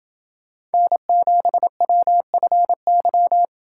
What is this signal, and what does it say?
Morse code, heard as a keyed tone, about 26 wpm, 700 Hz. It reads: N7WFY